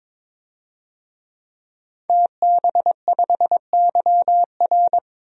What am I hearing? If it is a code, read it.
Morse code, heard as a keyed tone, about 22 wpm, 700 Hz. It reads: TB5YR
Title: DXCC-100 Call Signs - 5 Characters - 22wpm